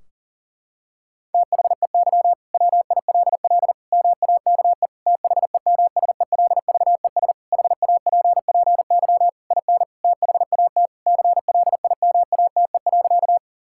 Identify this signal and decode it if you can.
Morse code, heard as a keyed tone, about 40 wpm, 700 Hz. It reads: THEY WILL MAKE THEMSELVES HAPPY IN THAT CLIMATE.
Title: Sentences from Top 1600 Words 40wpm